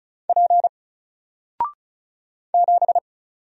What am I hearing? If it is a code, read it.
Morse code, heard as a keyed tone, about 35 wpm, 700 Hz. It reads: P 7
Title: Single Letter-Number Alt 35wpm